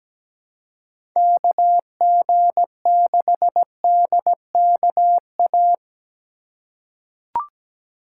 Morse code, heard as a keyed tone, about 17 wpm, 700 Hz. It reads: KG6DKA
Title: USA Call Signs - 6 Characters - 4x Character Spacing - 17wpm